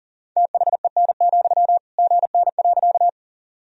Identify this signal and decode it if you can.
Morse code, heard as a keyed tone, about 40 wpm, 700 Hz. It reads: THEN, GN.
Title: Sentences from Top 300 Words and Abbreviations 40wpm